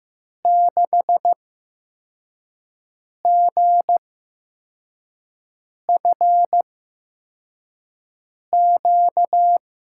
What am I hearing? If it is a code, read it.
Morse code, heard as a keyed tone, about 15 wpm, spaced as for 4 wpm, 700 Hz. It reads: 6GFQ
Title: USA Call Signs - 6 Characters - 8x Character Spacing - 15wpm